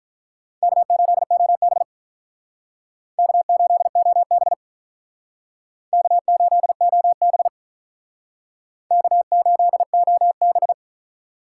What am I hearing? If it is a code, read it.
Morse code, heard as a keyed tone, about 45 wpm, 700 Hz. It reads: K8OB K8OB K8OB K8OB